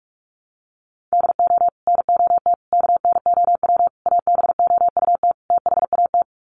Morse code, heard as a keyed tone, about 44 wpm, 700 Hz. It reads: DO NOT KNOW ABOUT THAT
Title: Sets of 5 Words Encore2 45wpm